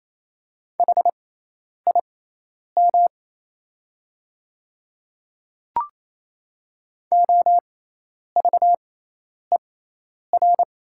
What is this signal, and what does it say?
Morse code, heard as a keyed tone, about 28 wpm, spaced as for 10 wpm, 700 Hz. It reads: HIM OVER
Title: 6x Character Spacing - Top 100 Words - 28wpm